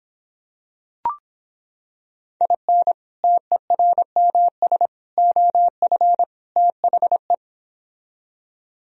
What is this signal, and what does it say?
Morse code, heard as a keyed tone, about 26 wpm, 700 Hz. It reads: IN TERMS OF THE